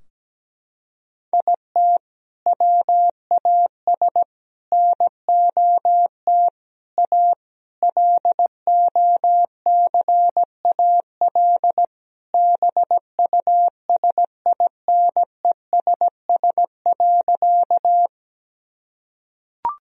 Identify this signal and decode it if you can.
Morse code, heard as a keyed tone, about 17 wpm, 700 Hz. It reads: IT WAS NOT A LOCAL BUSINESS.
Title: Sentences from Top 1000 Words - Review Part 3 - 17wpm